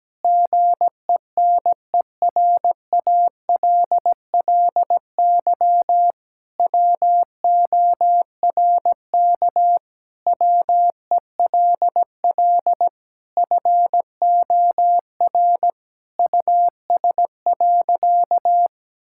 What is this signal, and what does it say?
Morse code, heard as a keyed tone, about 17 wpm, 700 Hz. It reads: GENERALLY WORK WELL FOR US.
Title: Sentences from Top 1000 Words - Review Part 6 - 17wpm